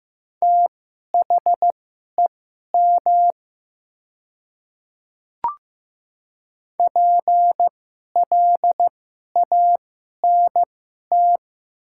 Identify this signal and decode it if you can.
Morse code, heard as a keyed tone, about 15 wpm, spaced as for 11 wpm, 700 Hz. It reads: THEM PLANT
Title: 2x Character Spacing - Top 300 Words - 15wpm